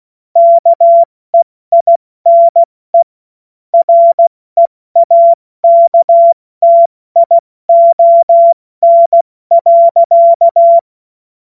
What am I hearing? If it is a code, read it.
Morse code, heard as a keyed tone, about 16 wpm, spaced as for 14 wpm, 680 Hz. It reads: KEINE REAKTION.
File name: Witz-02-September-2025-70-F80